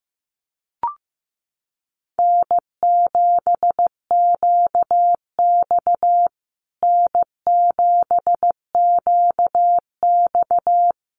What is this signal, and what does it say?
Morse code, heard as a keyed tone, about 15 wpm, 700 Hz. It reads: N7QX N7QX
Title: USA Call Signs Repeated 3x 15wpm